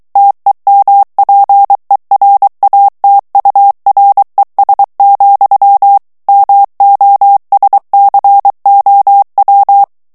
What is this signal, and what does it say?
Morse code, heard as a keyed tone, about 23 wpm, 800 Hz. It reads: TEMPERATURES, MOSCOW
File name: morse mp3 / ANGLAIS texte clair 2 vitesse 22 mots minute